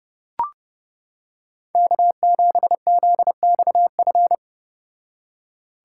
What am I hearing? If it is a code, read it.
Morse code, heard as a keyed tone, about 31 wpm, 700 Hz. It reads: K7ZXF